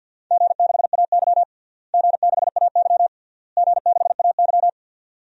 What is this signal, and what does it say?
Morse code, heard as a keyed tone, about 51 wpm, 700 Hz. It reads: G6AY G6AY G6AY
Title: DXCC-100 Call Signs - 4 Characters - Repeated 3x - 50wpm